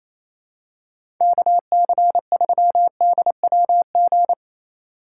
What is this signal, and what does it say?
Morse code, heard as a keyed tone, about 28 wpm, 700 Hz. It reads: KC3DWG
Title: USA Call Signs - 6 Characters - 8x Character Spacing - 28wpm